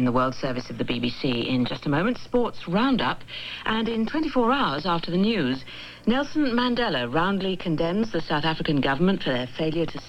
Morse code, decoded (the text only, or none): none